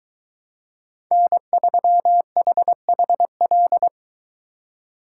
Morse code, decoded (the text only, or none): N3HHL